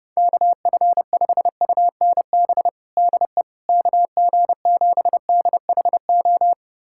Morse code, heard as a keyed tone, about 32 wpm, 700 Hz. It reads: KF5UNB DE KG7DHO
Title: USA Callsign DE Callsign - 6 Characters - 30wpm